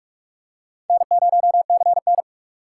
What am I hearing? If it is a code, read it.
Morse code, heard as a keyed tone, about 42 wpm, 700 Hz. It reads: N0CN